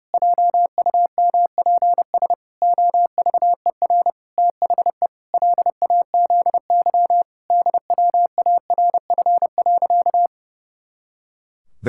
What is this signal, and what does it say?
Morse code, heard as a keyed tone, about 30 wpm, 700 Hz. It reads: JUMPS OVER THE LAZY DWARF.
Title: Pangram Challenge v2 - 30wpm